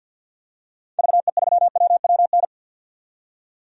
Morse code, heard as a keyed tone, about 48 wpm, 700 Hz. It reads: VE3WWN
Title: DXCC-100 Call Signs - 6 Characters - 50wpm